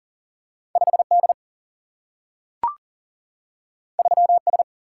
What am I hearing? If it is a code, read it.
Morse code, heard as a keyed tone, about 40 wpm, 700 Hz. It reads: 5D 3S
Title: Single-Letter Single-Number 40wpm